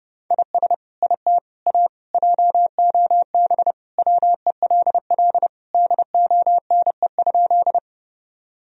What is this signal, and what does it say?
Morse code, heard as a keyed tone, about 30 wpm, 700 Hz. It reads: IS IT A JOB WELL DONE?
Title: Sentences from Top 1000 Words - Review Part 3 - 30wpm